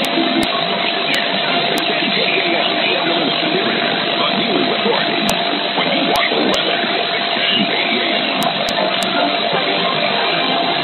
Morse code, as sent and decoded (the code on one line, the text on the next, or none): none